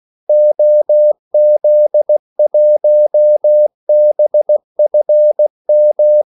Morse code, heard as a keyed tone, about 16 wpm, 600 Hz. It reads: OZ1BFM
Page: Kaldesignaler 16 wpm